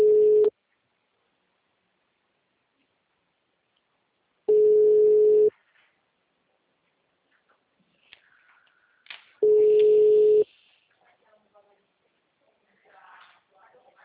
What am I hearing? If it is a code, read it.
Ringback tone (Europe, 425 Hz).